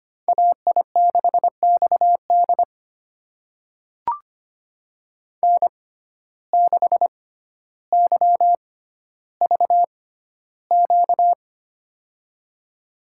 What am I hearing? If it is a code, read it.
Morse code, heard as a keyed tone, about 25 wpm, spaced as for 13 wpm, 700 Hz. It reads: AI6XD N6YVQ
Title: USA Call Signs - 5 Characters - 6x Character Spacing - 25wpm